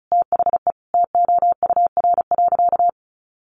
Morse code, heard as a keyed tone, about 34 wpm, 700 Hz. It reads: THE TOUR.
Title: Sentences from Top 1400 Words 35wpm